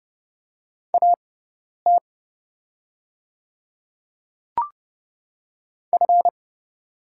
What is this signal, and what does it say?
Morse code, heard as a keyed tone, about 30 wpm, spaced as for 10 wpm, 700 Hz. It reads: AT F
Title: 6x Character Spacing - Top 200 Words - 30wpm